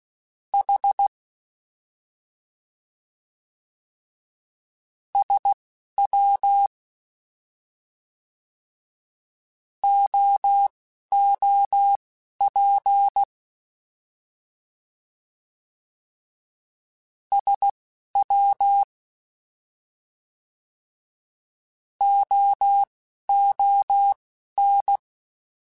Morse code, heard as a keyed tone, about 16 wpm, spaced as for 6 wpm, 800 Hz. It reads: H SWOOP SWOON